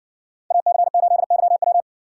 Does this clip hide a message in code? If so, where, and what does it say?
Yes: Morse code from 0.5 to 1.8 s, heard as a keyed tone, about 60 wpm, 700 Hz. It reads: AC8YW